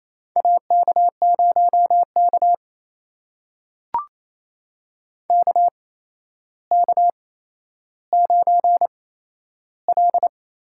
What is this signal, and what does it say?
Morse code, heard as a keyed tone, about 28 wpm, spaced as for 12 wpm, 700 Hz. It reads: AK0K KK9L
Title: USA Call Signs - 4 Characters - 8x Character Spacing - 28wpm